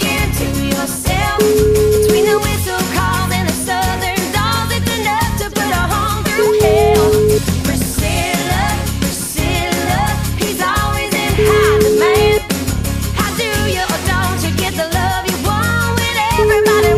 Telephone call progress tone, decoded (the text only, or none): ringback tone